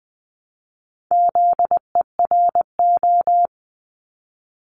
Morse code, heard as a keyed tone, about 20 wpm, 700 Hz. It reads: ZERO
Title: Multi-letter Phonemes Repeated 3x 20wpm